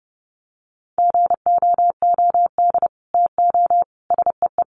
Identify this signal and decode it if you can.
Morse code, heard as a keyed tone, about 30 wpm, 700 Hz. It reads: GOOD TO SEE
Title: Sets of 5 Words Encore2 30wpm